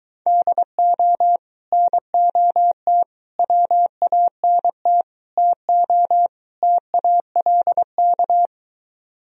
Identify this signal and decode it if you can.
Morse code, heard as a keyed tone, about 23 wpm, 700 Hz. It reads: DO NOT WANT TO TALK